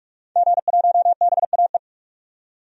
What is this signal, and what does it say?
Morse code, heard as a keyed tone, about 46 wpm, 700 Hz. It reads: G1BAE
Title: DXCC-100 Call Signs - 5 Characters - 45wpm